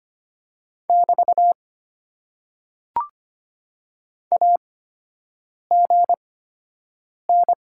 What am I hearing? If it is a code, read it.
Morse code, heard as a keyed tone, about 25 wpm, spaced as for 7 wpm, 700 Hz. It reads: <BT> AGN
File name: Top 90 QSO Elements - 8x Character Spacing - 25wpm